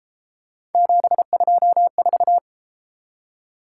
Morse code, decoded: 724

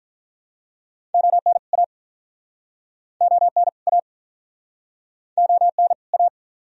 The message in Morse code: ON A ON A ON A

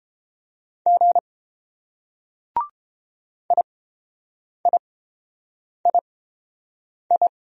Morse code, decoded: G IIII

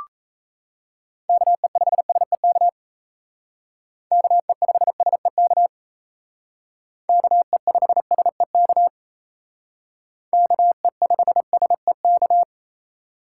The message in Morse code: KE5SEK KE5SEK KE5SEK KE5SEK

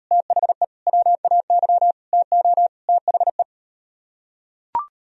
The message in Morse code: THE WAY TO THE

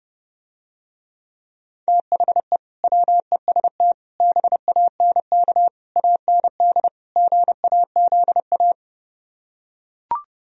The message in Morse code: THE WEST BANK AND GAZA

